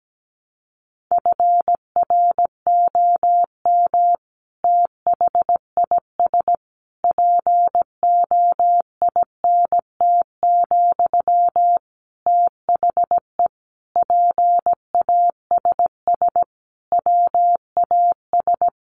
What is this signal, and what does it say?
Morse code, heard as a keyed tone, about 17 wpm, 700 Hz. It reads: FROM THIS POINT, THE PASS WAS
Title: Sentences from Top 1200 Words 17wpm